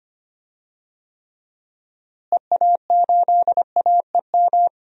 Morse code, heard as a keyed tone, about 25 wpm, 700 Hz. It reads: EA8AEM
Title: DXCC-100 Call Signs - 6 Characters - Repeated 3x - 25wpm